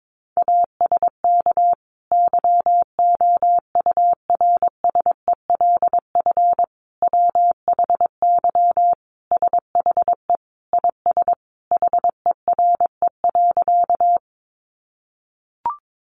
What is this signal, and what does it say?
Morse code, heard as a keyed tone, about 22 wpm, 700 Hz. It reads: ASK YOURSELF WHY SHE IS HERE.
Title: Sentences from Top 600 Words 22wpm